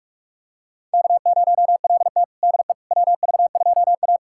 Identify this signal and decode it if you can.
Morse code, heard as a keyed tone, about 46 wpm, 700 Hz. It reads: K0LT DE WV2A
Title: USA Callsign DE Callsign - 4 Characters - 45wpm